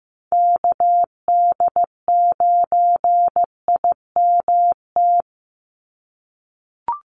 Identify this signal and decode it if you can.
Morse code, heard as a keyed tone, about 15 wpm, 700 Hz. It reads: KD9IMT
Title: USA Call Signs 15wpm